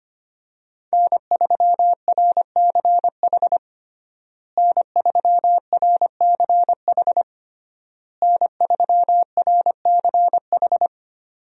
Morse code, heard as a keyed tone, about 25 wpm, 700 Hz. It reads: N3RCH N3RCH N3RCH
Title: USA Call Signs - 5 Characters - Repeated 3x - 25wpm